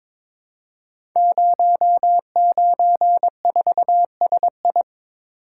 Morse code, heard as a keyed tone, about 22 wpm, 700 Hz. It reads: 094SI
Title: Code Groups - 5 Letter:Numbers - 22wpm